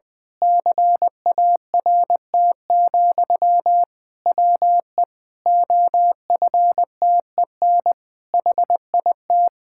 Morse code, heard as a keyed tone, about 19 wpm, 700 Hz. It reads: CART, WE OFTEN HIT
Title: Pangram Challenge v2 - 20wpm